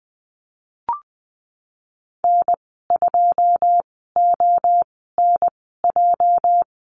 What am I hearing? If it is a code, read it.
Morse code, heard as a keyed tone, about 20 wpm, spaced as for 15 wpm, 700 Hz. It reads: N2ONJ
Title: 231 Lesson 37g - Letter x - Call Signs 20wpm